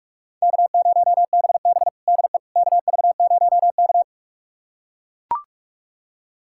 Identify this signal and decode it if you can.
Morse code, heard as a keyed tone, about 42 wpm, 700 Hz. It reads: K0BB DE KV0K
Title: USA Callsign DE Callsign - 4 Characters - 45wpm